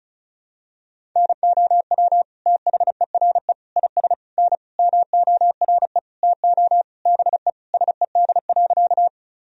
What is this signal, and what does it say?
Morse code, heard as a keyed tone, about 36 wpm, 700 Hz. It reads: NOW THERE IS N MORE TO BE SED.
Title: Sentences from Top 100 Words and Abbreviations 35wpm